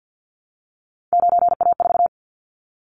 Morse code, heard as a keyed tone, about 50 wpm, 700 Hz. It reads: 9A4
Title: Callsign Prefixes for Top 50 DXCC - 50wpm